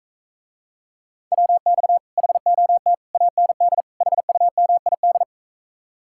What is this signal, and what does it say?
Morse code, heard as a keyed tone, about 44 wpm, 700 Hz. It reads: WX HOT AND HUMID